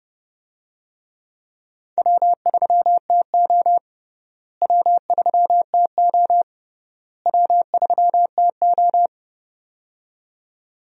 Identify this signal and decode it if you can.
Morse code, heard as a keyed tone, about 30 wpm, 700 Hz. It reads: W3TO W3TO W3TO
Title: USA Call Signs - 4 Characters - Repeated 3x - 30wpm